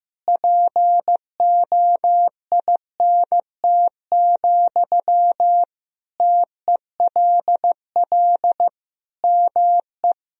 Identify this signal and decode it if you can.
Morse code, heard as a keyed tone, about 16 wpm, 700 Hz. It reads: POINT, TELL ME
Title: Sentences from Top 1000 Words - Review Part 1 - 15wpm